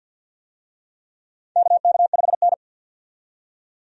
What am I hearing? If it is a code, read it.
Morse code, heard as a keyed tone, about 52 wpm, 700 Hz. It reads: KK5N
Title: USA Call Signs - 4 Characters - 50wpm